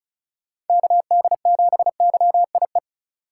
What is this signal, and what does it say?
Morse code, heard as a keyed tone, about 34 wpm, 700 Hz. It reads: KD7YIE
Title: USA Call Signs - 6 Characters - 3x Character Spacing - 35wpm